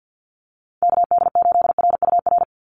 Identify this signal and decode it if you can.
Morse code, heard as a keyed tone, about 50 wpm, 700 Hz. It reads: KD8RUR
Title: USA Call Signs 50wpm